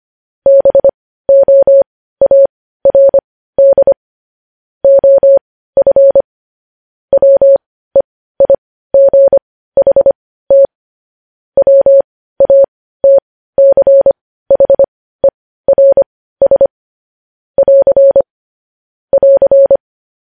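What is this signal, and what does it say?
Morse code, heard as a keyed tone, about 25 wpm, spaced as for 15 wpm, 550 Hz. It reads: BOARD OF WEIGHT WATCHERS <AR> <AR>
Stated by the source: Morse news headlines sent at 15 words per minute.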